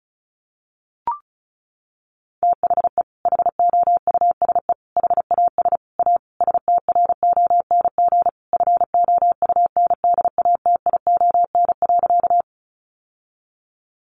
Morse code, heard as a keyed tone, about 34 wpm, 700 Hz. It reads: THE HOUSE HAS A STRONG FOUNDATION.
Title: Sentences from Top 1200 Words 35wpm